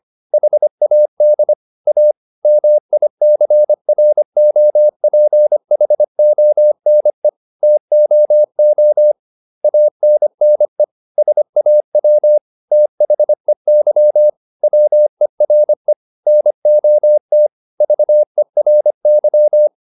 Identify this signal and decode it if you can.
Morse code, heard as a keyed tone, about 25 wpm, 600 Hz. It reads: HAD A MICROPHONE TOO ANNE SAW THEY WERE NOT VERY